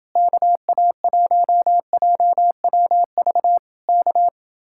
Morse code, heard as a keyed tone, about 29 wpm, 700 Hz. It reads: KA1JWV K